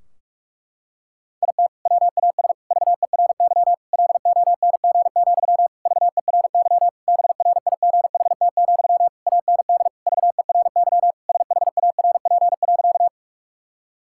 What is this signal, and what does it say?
Morse code, heard as a keyed tone, about 42 wpm, 700 Hz. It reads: IT WAS VERY LONG, VERY BRIGHT, AND VERY SHARP.
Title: Sentences from Top 1600 Words 45wpm